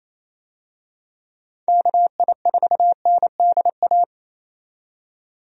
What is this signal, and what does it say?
Morse code, heard as a keyed tone, about 28 wpm, 700 Hz. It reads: KI4NDA
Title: USA Call Signs - 6 Characters - 28wpm